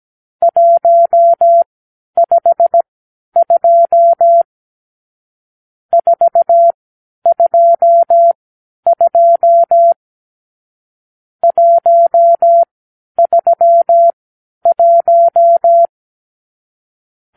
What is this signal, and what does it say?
Morse code, heard as a keyed tone, about 17 wpm, spaced as for 10 wpm, 680 Hz. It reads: 152 422 131
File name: arrl / Review (1-5)